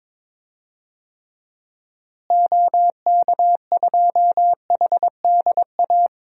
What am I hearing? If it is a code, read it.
Morse code, heard as a keyed tone, about 22 wpm, 700 Hz. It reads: OK2HDA
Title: DXCC-100 Call Signs - 6 Characters - Repeated 3x - 22wpm